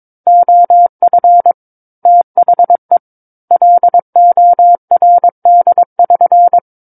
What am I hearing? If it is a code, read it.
Morse code, heard as a keyed tone, about 22 wpm, 700 Hz. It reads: OF THE LORD<SN>
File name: Book One / 13-Chapter 13 - 22 WPM